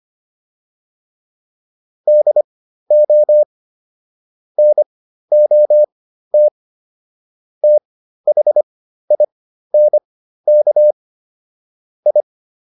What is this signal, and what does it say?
Morse code, heard as a keyed tone, about 25 wpm, spaced as for 13 wpm, 600 Hz. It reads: DO NOT THINK I